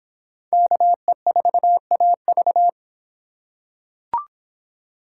Morse code, heard as a keyed tone, about 23 wpm, 700 Hz. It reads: KE4AV